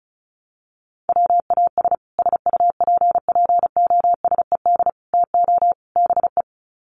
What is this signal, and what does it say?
Morse code, heard as a keyed tone, about 34 wpm, 700 Hz. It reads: WAS SUPPOSED TO BE
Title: Sets of 4 Words 35wpm